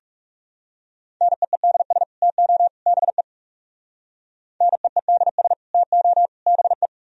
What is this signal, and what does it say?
Morse code, heard as a keyed tone, about 37 wpm, 700 Hz. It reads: NEEDS TO BE NEEDS TO BE
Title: Speed-Racing - Sets of 3 Words 30wpm